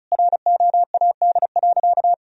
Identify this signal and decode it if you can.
Morse code, heard as a keyed tone, about 36 wpm, 700 Hz. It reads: ROAD.
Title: Sentences from Top 400 Words and Abbreviations 35wpm